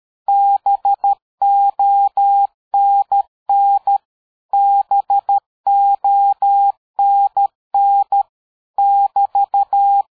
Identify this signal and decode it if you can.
Morse code, heard as a keyed tone, about 13 wpm, 800 Hz. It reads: BONN BONN <BT>
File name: dk5ke / QSO 60BpM